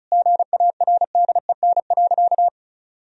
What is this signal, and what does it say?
Morse code, heard as a keyed tone, about 36 wpm, 700 Hz. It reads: GARDEN.
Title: Sentences from Top 1000 Words - Review Part 4 - 35wpm